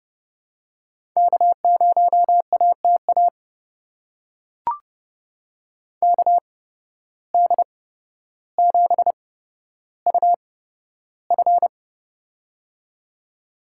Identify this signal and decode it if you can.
Morse code, heard as a keyed tone, about 30 wpm, spaced as for 13 wpm, 700 Hz. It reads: K0ATA KD7UF
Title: USA Call Signs - 8x Character Spacing - 30wpm